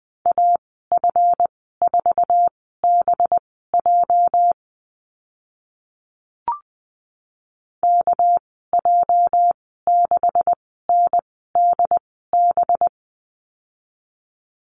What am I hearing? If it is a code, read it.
Morse code, heard as a keyed tone, about 20 wpm, spaced as for 15 wpm, 700 Hz. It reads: AF4BJ KJ6NDB
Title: 231 Lesson 37g - Letter x - Call Signs 20wpm